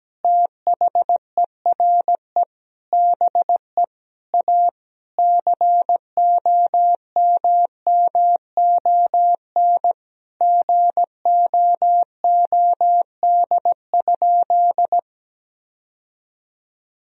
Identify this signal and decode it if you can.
Morse code, heard as a keyed tone, about 17 wpm, 700 Hz. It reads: THERE BE A COMMON GOOD?
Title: Sentences from Top 1000 Words - Review Part 3 - 17wpm